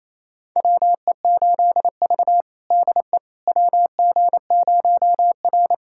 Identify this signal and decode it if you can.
Morse code, heard as a keyed tone, about 28 wpm, 700 Hz. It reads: WE8V DE WG0R
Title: USA Callsign DE Callsign - 4 Characters - 28wpm